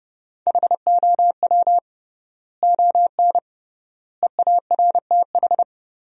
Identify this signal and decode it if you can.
Morse code, heard as a keyed tone, about 30 wpm, 700 Hz. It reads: HOW ON EARTH
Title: 3x Word-Spacing - Sentences from Top 200 Words - 30wpm